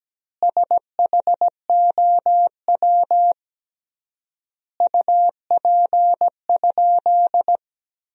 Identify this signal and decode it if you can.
Morse code, heard as a keyed tone, about 17 wpm, 700 Hz. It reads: SHOW UP?
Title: 3x Word-Spacing - Sentences from Top 200 Words - 17wpm